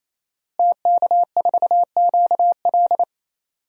Morse code, heard as a keyed tone, about 27 wpm, 700 Hz. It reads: TK4QL
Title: DXCC-100 Call Signs - 5 Characters - Repeated 3x - 28wpm